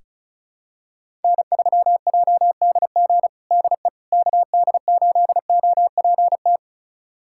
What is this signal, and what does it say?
Morse code, heard as a keyed tone, about 36 wpm, 700 Hz. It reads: N3JDG DE KD8OPT
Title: USA Callsign DE Callsign 35wpm